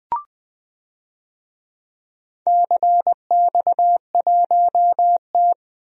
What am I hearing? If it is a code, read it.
Morse code, heard as a keyed tone, about 21 wpm, 700 Hz. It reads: CX1T
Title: DXCC-100 Call Signs - 4 Characters - Repeated 2x - 20wpm